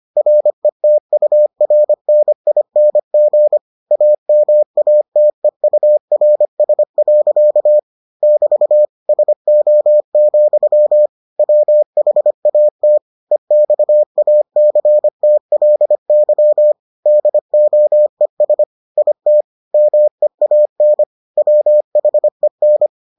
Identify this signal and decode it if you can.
Morse code, heard as a keyed tone, about 25 wpm, 600 Hz. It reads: RETURNING AMATEURS. <BT> SO, WHAT EXACTLY DOES IT MEAN WHEN